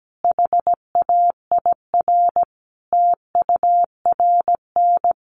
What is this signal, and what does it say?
Morse code, heard as a keyed tone, about 17 wpm, 700 Hz. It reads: HAIR TURN
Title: A Sound Of Thunder 17wpm